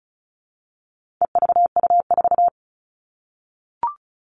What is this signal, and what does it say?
Morse code, heard as a keyed tone, about 34 wpm, 700 Hz. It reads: EVU4
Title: All Cut Numbers - 4-Digit 35wpm